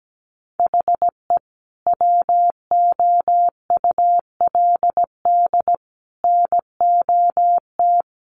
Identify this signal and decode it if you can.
Morse code, heard as a keyed tone, about 17 wpm, 700 Hz. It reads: HE WOULD NOT
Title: Sentences from Top 1300 Words 17wpm